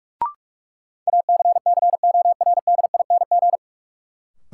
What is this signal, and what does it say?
Morse code, heard as a keyed tone, about 47 wpm, 700 Hz. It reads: ACCORDING
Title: Sentences from Top 700 Words - New Word Drill - 45wpm